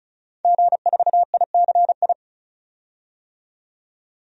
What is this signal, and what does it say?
Morse code, heard as a keyed tone, about 36 wpm, 700 Hz. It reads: G4ICI
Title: DXCC-100 Call Signs - 5 Characters - 35wpm